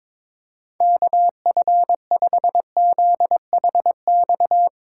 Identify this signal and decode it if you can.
Morse code, heard as a keyed tone, about 22 wpm, 700 Hz. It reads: KF5ZHX
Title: Speed Racing - USA 2x3 Call Signs 22wpm